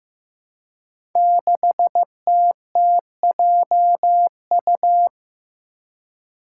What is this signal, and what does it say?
Morse code, heard as a keyed tone, about 15 wpm, 700 Hz. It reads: 6TTJU
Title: Five Challenging Letter-Number Combinations 15wpm